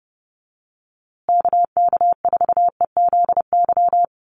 Morse code, heard as a keyed tone, about 30 wpm, 700 Hz. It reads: KK4EZY
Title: USA Call Signs Repeated 2x 30wpm